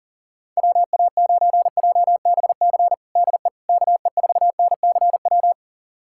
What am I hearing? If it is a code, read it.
Morse code, heard as a keyed tone, about 37 wpm, 700 Hz. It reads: WA9JBC DE KE4NCW
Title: USA Callsign DE Callsign - 6 Characters - 40wpm